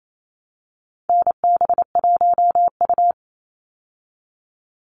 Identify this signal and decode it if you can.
Morse code, heard as a keyed tone, about 28 wpm, 700 Hz. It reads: NB1U
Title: Call Signs Repeated 2x - 28wpm